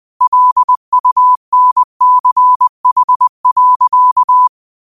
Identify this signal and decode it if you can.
Morse code, heard as a keyed tone, about 20 wpm, 1000 Hz. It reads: LUNCH.